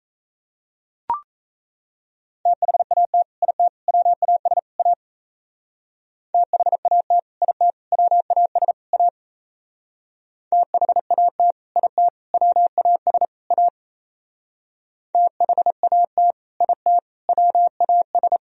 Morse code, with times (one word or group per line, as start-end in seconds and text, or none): 2.5-3.2 THAT
3.4-3.7 IT
3.9-4.6 WAS
4.8-4.9 A
6.3-7.2 THAT
7.4-7.7 IT
7.9-8.7 WAS
8.9-9.1 A
10.5-11.5 THAT
11.8-12.1 IT
12.3-13.3 WAS
13.5-13.7 A
15.1-16.3 THAT
16.6-17.0 IT
17.3-18.4 WAS